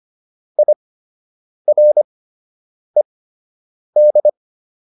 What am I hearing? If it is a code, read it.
Morse code, heard as a keyed tone, about 25 wpm, spaced as for 8 wpm, 600 Hz. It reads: IRED